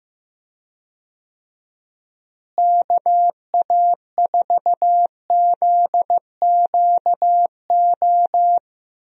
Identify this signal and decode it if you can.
Morse code, heard as a keyed tone, about 15 wpm, 700 Hz. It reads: KA4ZQO
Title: USA Call Signs - 6 Characters - Repeated 3x - 15wpm